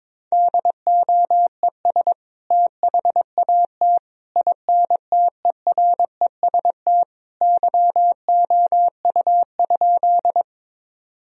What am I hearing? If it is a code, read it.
Morse code, heard as a keyed tone, about 22 wpm, 700 Hz. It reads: DOES THAT INTEREST YOU?
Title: Sentences from Top 1000 Words - Review Part 2 - 22wpm